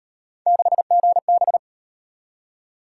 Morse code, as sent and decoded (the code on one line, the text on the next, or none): -.... --. -...
6GB